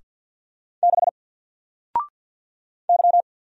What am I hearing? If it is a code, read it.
Morse code, heard as a keyed tone, about 50 wpm, 700 Hz. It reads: 6 Y
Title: Single Character (Letter:Number:Punctuation:Prosign) 50wpm